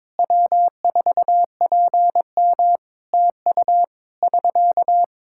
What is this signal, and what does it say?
Morse code, heard as a keyed tone, about 22 wpm, 700 Hz. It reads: W4PM TU <SK>
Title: K1USN SST QSO Practice 22wpm